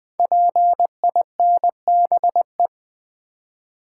Morse code, heard as a keyed tone, about 20 wpm, 700 Hz. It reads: PINBE
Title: Code Groups - 5 Letter:Numbers - 20wpm